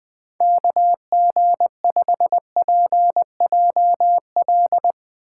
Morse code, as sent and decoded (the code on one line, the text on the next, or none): -.- --. ..... .--. .--- .-..
KG5PJL